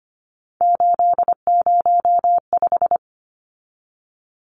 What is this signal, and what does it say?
Morse code, heard as a keyed tone, about 25 wpm, 700 Hz. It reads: 805
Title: Numbers - 3-Digit 25wpm